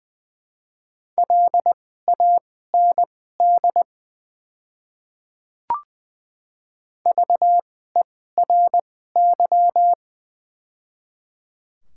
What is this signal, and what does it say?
Morse code, heard as a keyed tone, about 20 wpm, spaced as for 14 wpm, 700 Hz. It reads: LAND VERY
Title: 2x Character Spacing - Top 200 Words - 20wpm